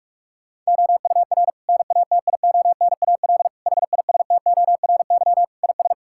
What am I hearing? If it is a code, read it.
Morse code, heard as a keyed tone, about 46 wpm, 700 Hz. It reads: OUR NATIONAL HISTORY IS
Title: Sentences from Top 1000 Words - Review Part 6 - 45wpm